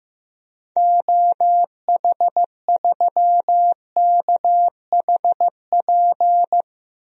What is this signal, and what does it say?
Morse code, heard as a keyed tone, about 15 wpm, 700 Hz. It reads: OH3KHP
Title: DXCC-100 Call Signs - 6 Characters - 15wpm